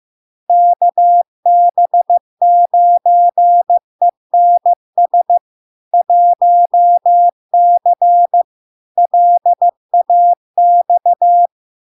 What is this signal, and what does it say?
Morse code, heard as a keyed tone, about 15 wpm, 700 Hz. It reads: KB9ENS 1C LAX
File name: LARK-FD-2022-Log-in-CW-15wpm-700-Hz